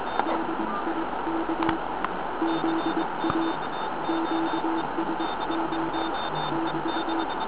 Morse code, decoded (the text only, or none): DIK ZA QSO <BT>